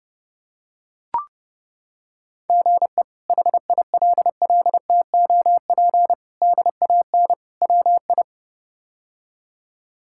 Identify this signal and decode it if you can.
Morse code, heard as a keyed tone, about 30 wpm, 700 Hz. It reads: GE HILLTOP DAN WI
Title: K1USN SST QSO Practice - 30wpm